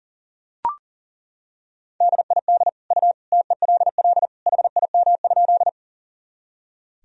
Morse code, heard as a keyed tone, about 37 wpm, 700 Hz. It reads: DID U TELL HIM?
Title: Sentences from Top 200 Words and Abbreviations 40wpm